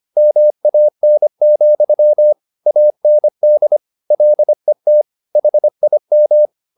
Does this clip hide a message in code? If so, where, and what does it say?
Yes: Morse code from 0.2 to 6.5 s, heard as a keyed tone, about 25 wpm, 600 Hz. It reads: MAN, AND LET HIM